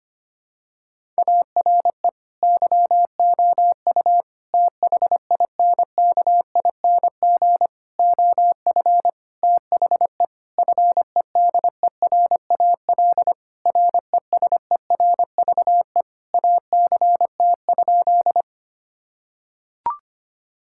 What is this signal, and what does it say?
Morse code, heard as a keyed tone, about 25 wpm, 700 Hz. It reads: ARE YOU THINKING OF THE FEDERAL RESERVE ACT?
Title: Sentences from Top 1000 Words - Review Part 6 - 25wpm